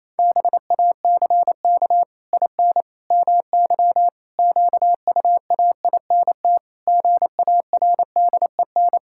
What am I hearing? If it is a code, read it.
Morse code, heard as a keyed tone, about 27 wpm, 700 Hz. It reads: BACK IN MY QUAINT GARDEN
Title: Pangram Challenge v2 - 28wpm